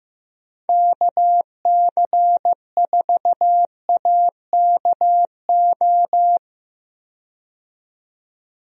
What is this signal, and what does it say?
Morse code, heard as a keyed tone, about 15 wpm, 700 Hz. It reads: KC4AKO
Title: USA Call Signs - 6 Characters - 15wpm